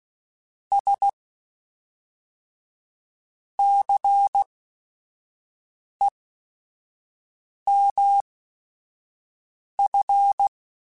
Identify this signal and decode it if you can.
Morse code, heard as a keyed tone, about 16 wpm, spaced as for 5 wpm, 800 Hz. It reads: S CEMF
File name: Technical Subjects / Technical Subjects-mp3s / XL @5